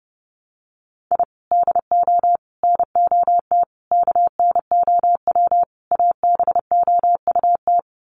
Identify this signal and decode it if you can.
Morse code, heard as a keyed tone, about 30 wpm, 700 Hz. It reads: I DO NOT KNOW ABOUT
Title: Sets of 5 Words 30wpm